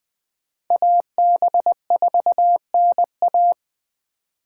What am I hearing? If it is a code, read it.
Morse code, heard as a keyed tone, about 25 wpm, 700 Hz. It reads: AB4NA